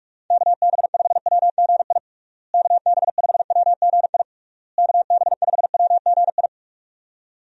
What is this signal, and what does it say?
Morse code, heard as a keyed tone, about 41 wpm, 700 Hz. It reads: KB5WGI KB5WGI KB5WGI
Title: USA Call Signs - 6 Characters - Repeated 3x - 45wpm